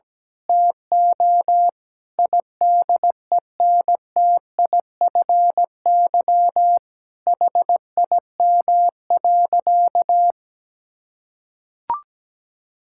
Morse code, heard as a keyed tone, about 17 wpm, 700 Hz. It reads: TO IDENTIFY HIM.
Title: Sentences from Top 1000 Words - Review Part 6 - 17wpm